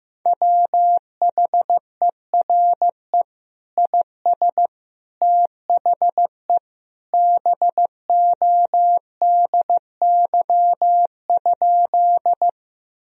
Morse code, heard as a keyed tone, about 15 wpm, 700 Hz. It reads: WHERE IS THE BODY?
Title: Sentences from Top 1000 Words - Review Part 2 - 15wpm